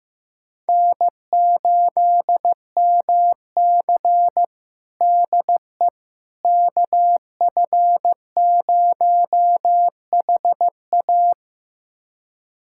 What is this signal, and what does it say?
Morse code, heard as a keyed tone, about 16 wpm, 700 Hz. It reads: N8MC DE KF0HA
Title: USA Callsign DE Callsign 15wpm